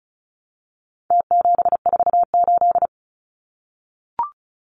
Morse code, heard as a keyed tone, about 34 wpm, 700 Hz. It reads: T748
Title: Numbers - Contest Style - 4-Digit 35wpm